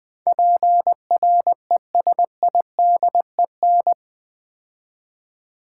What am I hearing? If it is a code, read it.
Morse code, heard as a keyed tone, about 20 wpm, 700 Hz. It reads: PRESIDEN
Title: Build A Word - 4 Letters At A Time - 12-Letter Words - 20wpm